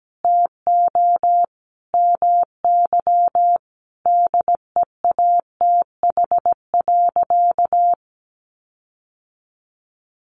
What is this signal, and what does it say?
Morse code, heard as a keyed tone, about 17 wpm, 700 Hz. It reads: TO MY DEATH.
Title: Sentences from Top 1500 Words 17wpm